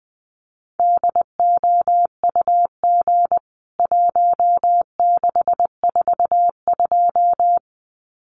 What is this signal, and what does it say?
Morse code, heard as a keyed tone, about 20 wpm, 700 Hz. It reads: DOUG 1642
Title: CWT Exchange 20wpm